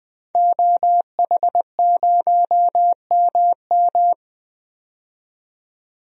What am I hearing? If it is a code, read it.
Morse code, heard as a keyed tone, about 20 wpm, 700 Hz. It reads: OH0MM
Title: DXCC-100 Call Signs - 5 Characters - 20wpm